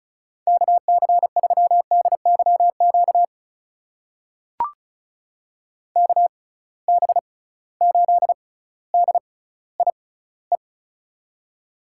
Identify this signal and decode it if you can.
Morse code, heard as a keyed tone, about 36 wpm, spaced as for 18 wpm, 700 Hz. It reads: KC3DYQ KB8DIE
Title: USA Call Signs - 6x Character Spacing - 35wpm